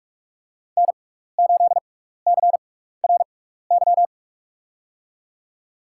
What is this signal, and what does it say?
Morse code, heard as a keyed tone, about 46 wpm, spaced as for 15 wpm, 700 Hz. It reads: N8CRY
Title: USA Call Signs - 5 Characters - 6x Character Spacing - 45wpm